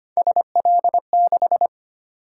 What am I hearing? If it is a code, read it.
Morse code, heard as a keyed tone, about 25 wpm, 700 Hz. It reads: SL6
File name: Three Letter-Number Combinations - 25wpm